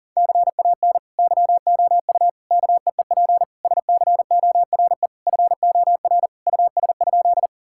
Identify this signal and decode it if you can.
Morse code, heard as a keyed tone, about 40 wpm, 700 Hz. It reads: CAN YOU KEEP SCORE FOR US?
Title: Sentences from Top 1600 Words 40wpm